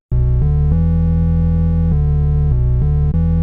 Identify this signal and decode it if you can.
There is no code.